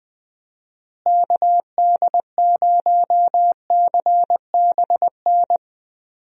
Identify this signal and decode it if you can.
Morse code, heard as a keyed tone, about 21 wpm, 700 Hz. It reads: KD0CBN
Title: USA Call Signs - 6 Characters - Repeated 3x - 20wpm